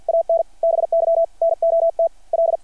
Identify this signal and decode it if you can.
Morse code, heard as a keyed tone, about 52 wpm, 660 Hz. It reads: AN BY NOT L
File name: archived / CRSnet